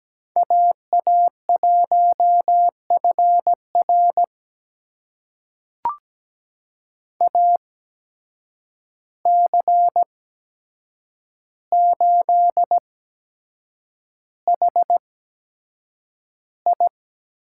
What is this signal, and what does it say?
Morse code, heard as a keyed tone, about 17 wpm, spaced as for 7 wpm, 700 Hz. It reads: AA1FR AC8HI
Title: USA Call Signs - 5 Characters - 8x Character Spacing - 17wpm